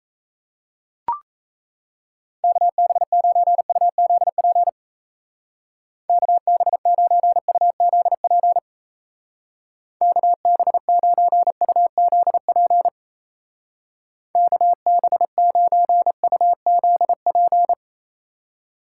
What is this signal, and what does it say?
Morse code, heard as a keyed tone, about 34 wpm, 700 Hz. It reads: KB9UZP KB9UZP KB9UZP KB9UZP
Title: Speed Racing - USA 2x3 Call Signs 28wpm